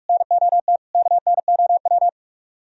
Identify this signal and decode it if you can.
Morse code, heard as a keyed tone, about 55 wpm, 700 Hz. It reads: NOT KNOW
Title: Speed-Racing - Sets of 4 Words 45wpm